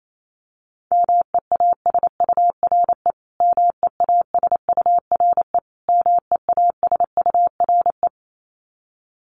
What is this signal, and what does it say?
Morse code, heard as a keyed tone, about 28 wpm, 700 Hz. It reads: MEASURE MEASURE MEASURE
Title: Top 500 Words Repeated 28wpm